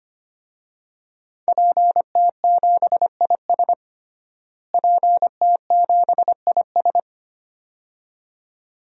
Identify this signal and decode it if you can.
Morse code, heard as a keyed tone, about 25 wpm, 700 Hz. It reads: PT7IS PT7IS
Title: DXCC-100 Call Signs - 5 Characters - Repeated 2x - 25wpm